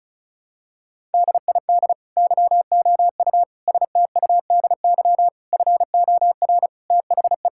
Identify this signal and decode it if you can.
Morse code, heard as a keyed tone, about 36 wpm, 700 Hz. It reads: DID YOU STUDY FOR THE
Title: Sentences from Top 1000 Words - Review Part 5 - 35wpm